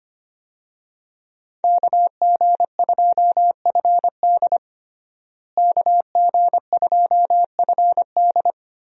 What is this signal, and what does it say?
Morse code, heard as a keyed tone, about 25 wpm, 700 Hz. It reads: KG2FD KG2FD
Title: USA Call Signs - 5 Characters - Repeated 2x - 25wpm